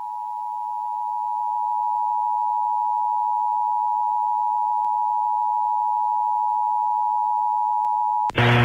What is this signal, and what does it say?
Emergency Alert System (SAME) tones (the 853 and 960 Hz attention tone). It is heard from 0.4 to 8.2 s.